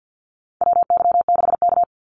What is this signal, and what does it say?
Morse code, heard as a keyed tone, about 50 wpm, 700 Hz. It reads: WY6K
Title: Call Signs - 50wpm